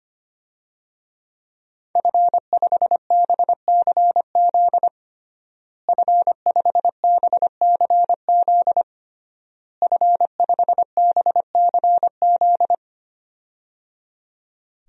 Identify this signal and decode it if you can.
Morse code, heard as a keyed tone, about 25 wpm, 700 Hz. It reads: F5BCZ F5BCZ F5BCZ
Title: DXCC-100 Call Signs - 5 Characters - Repeated 3x - 25wpm